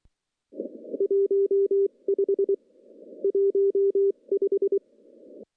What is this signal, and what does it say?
Morse code, heard as a keyed tone, about 23 wpm, 2000 Hz, with noise. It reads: 1 5 15
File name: CW-SS-k5tr